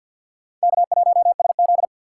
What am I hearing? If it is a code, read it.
Morse code, heard as a keyed tone, about 65 wpm, 700 Hz. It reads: K1SZ